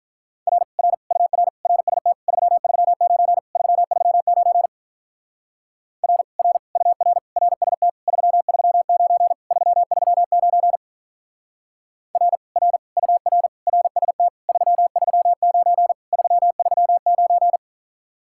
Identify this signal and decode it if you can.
Morse code, heard as a keyed tone, about 43 wpm, 700 Hz. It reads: R R UR RST 339 339 R R UR RST 339 339 R R UR RST 339 339
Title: Speed-Racing Common QSO Phrases - 35wpm